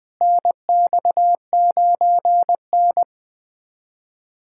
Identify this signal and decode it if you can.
Morse code, heard as a keyed tone, about 20 wpm, 700 Hz. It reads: NX9N
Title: USA Callsign DE Callsign - 4 Characters - 20wpm